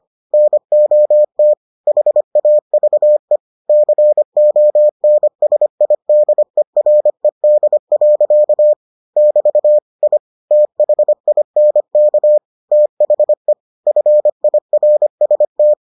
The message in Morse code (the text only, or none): NOT HAVE CONSIDERED. <BT> I THINK THE FIRST